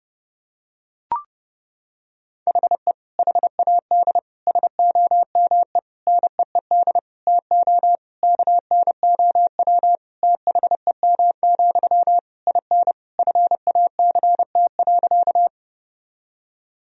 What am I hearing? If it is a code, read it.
Morse code, heard as a keyed tone, about 30 wpm, 700 Hz. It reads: HE HAD SOME NEED TO KNOW THEM, IN FACT.
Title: Sentences from Top 1000 Words - Review Part 3 - 30wpm